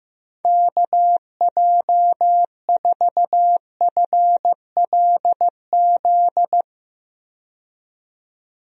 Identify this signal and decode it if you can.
Morse code, heard as a keyed tone, about 15 wpm, 700 Hz. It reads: KJ4FLZ
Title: USA Call Signs - 6 Characters - 15wpm